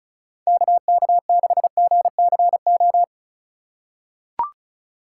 Morse code, heard as a keyed tone, about 36 wpm, 700 Hz. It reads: KK6GCO
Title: USA Call Signs - 6 Characters - 3x Character Spacing - 35wpm